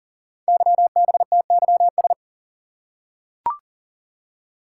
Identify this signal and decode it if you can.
Morse code, heard as a keyed tone, about 40 wpm, 700 Hz. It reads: YBTYS
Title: Code Groups - 5 Letters - 40wpm